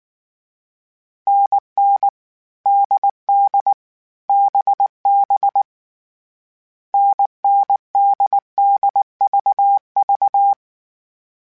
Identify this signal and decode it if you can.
Morse code, heard as a keyed tone, about 19 wpm, 800 Hz. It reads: NN DD BB NNDDVV